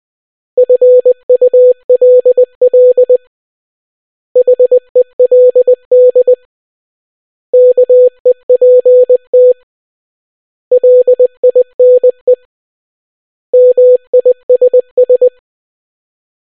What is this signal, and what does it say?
Morse code, heard as a keyed tone, about 20 wpm, 500 Hz. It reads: FULL HELD KEPT LINE MISS